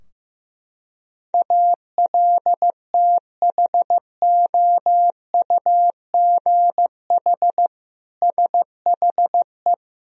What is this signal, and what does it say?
Morse code, heard as a keyed tone, about 15 wpm, 700 Hz. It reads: ALTHOUGH SHE
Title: Sentences from Top 1000 Words - Review Part 3 - 15wpm